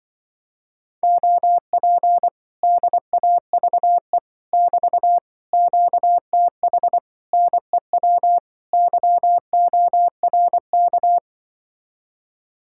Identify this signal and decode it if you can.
Morse code, heard as a keyed tone, about 24 wpm, 700 Hz. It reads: OP DAVE <BT> QTH NEW YORK
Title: Speed-Racing Common QSO Phrases - 20wpm